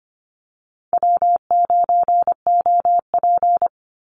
Morse code, heard as a keyed tone, about 25 wpm, 700 Hz. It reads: W9OP
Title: Call Signs Repeated 3x - 25wpm